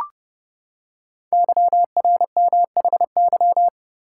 Morse code, heard as a keyed tone, about 30 wpm, 700 Hz. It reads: YRMHY
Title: Code Groups - 5 Letters - 30wpm